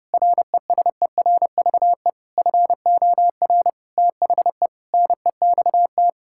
Morse code, heard as a keyed tone, about 30 wpm, 700 Hz. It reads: RESERVE FOR THE NEXT
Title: Sentences from Top 1000 Words - Review Part 6 - 30wpm